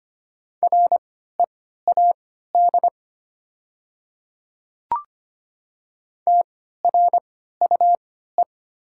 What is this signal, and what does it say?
Morse code, heard as a keyed tone, about 25 wpm, spaced as for 14 wpm, 700 Hz. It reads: READ TRUE
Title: 3x Character Spacing - Top 500 Words - 25wpm